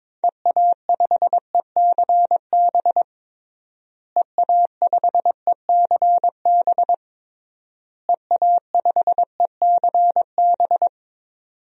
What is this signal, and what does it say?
Morse code, heard as a keyed tone, about 22 wpm, 700 Hz. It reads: EA5ECB EA5ECB EA5ECB
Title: DXCC-100 Call Signs - 6 Characters - Repeated 3x - 22wpm